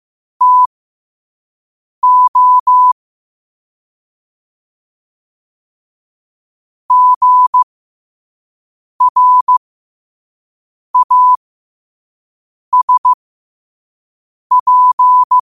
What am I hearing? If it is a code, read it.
Morse code, heard as a keyed tone, about 15 wpm, spaced as for 5 wpm, 1000 Hz. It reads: TO GRASP
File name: qotdfiles / 5wpm